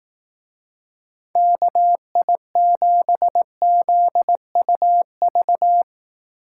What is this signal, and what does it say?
Morse code, heard as a keyed tone, about 19 wpm, 700 Hz. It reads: KI7ZUV
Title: Speed Racing - USA 2x3 Call Signs 15wpm